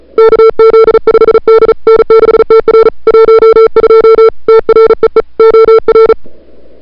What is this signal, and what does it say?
Morse code, heard as a keyed tone, about 34 wpm, 440 Hz. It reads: KZ5D N6TR 12 TREE OR
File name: CWSprint / February 2007 CW Sprint / 01-2007 CW Sprint Hour1